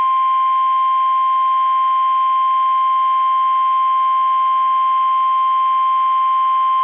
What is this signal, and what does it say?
Emergency Alert System (SAME) tones (the 1050 Hz attention tone). It is heard from 0.4 to 6.0 s.
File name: eas / KUSO / KUSO